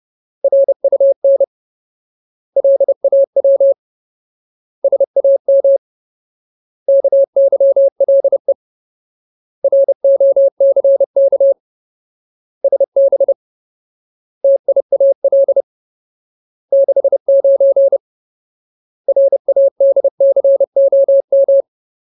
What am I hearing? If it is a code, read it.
Morse code, heard as a keyed tone, about 30 wpm, 550 Hz. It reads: RUN LAW SAM KYLE ROCK SB TIAL 69 RADCOM
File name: Ultimate Word List / 30WPM Ultimate